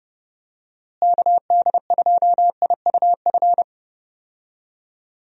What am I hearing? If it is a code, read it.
Morse code, heard as a keyed tone, about 30 wpm, 700 Hz. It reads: KD2IUF
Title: USA Call Signs - 6 Characters - Repeated 2x - 30wpm